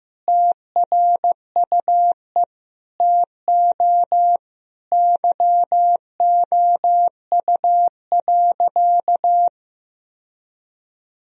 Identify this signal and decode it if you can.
Morse code, heard as a keyed tone, about 15 wpm, 700 Hz. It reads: TRUE TO YOU.
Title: Sentences from Top 1000 Words - Review Part 4 - 15wpm